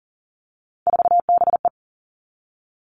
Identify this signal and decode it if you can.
Morse code, heard as a keyed tone, about 40 wpm, 700 Hz. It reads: 4BE
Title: All Cut Numbers - 3-Digit 40wpm